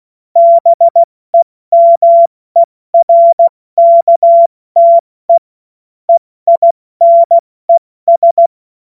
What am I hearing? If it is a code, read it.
Morse code, heard as a keyed tone, about 16 wpm, spaced as for 14 wpm, 680 Hz. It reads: BEMERKTE EINES
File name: Morsekurs / Witze / Witz 05 Dezember 2023 70 F80